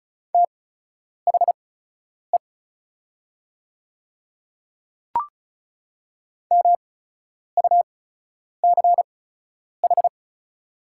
Morse code, heard as a keyed tone, about 36 wpm, spaced as for 10 wpm, 700 Hz. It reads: THE MUCH
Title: 8x Character Spacing - Top 200 Words - 35wpm